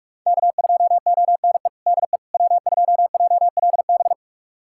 Morse code, heard as a keyed tone, about 46 wpm, 700 Hz. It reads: K2ONE DE W2JLB
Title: USA Callsign DE Callsign - 5 Characters - 45wpm